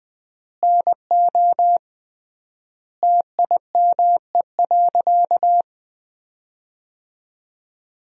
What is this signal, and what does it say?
Morse code, heard as a keyed tone, about 20 wpm, 700 Hz. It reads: NO TIME.
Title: 3x Word-Spacing - Sentences from Top 400 Words - 20wpm